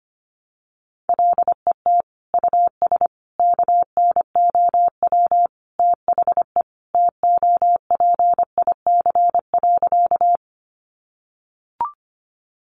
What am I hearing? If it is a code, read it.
Morse code, heard as a keyed tone, about 24 wpm, 700 Hz. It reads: LET US KNOW THE TOPIC.
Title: Sentences from Top 1200 Words 25wpm